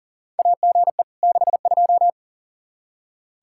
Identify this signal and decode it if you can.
Morse code, heard as a keyed tone, about 40 wpm, 700 Hz. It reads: AGE 62
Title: Practice QSO - Exchange 3 - More Info - 40wpm